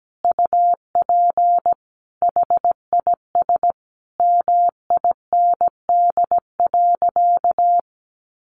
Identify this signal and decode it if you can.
Morse code, heard as a keyed tone, about 17 wpm, 700 Hz. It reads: UP HIS MIND.
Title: Sentences from Top 500 Words 17wpm